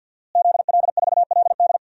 Morse code, heard as a keyed tone, about 50 wpm, 700 Hz. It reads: ZL4LD
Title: DXCC-100 Call Signs - 5 Characters - Repeated 3x - 50wpm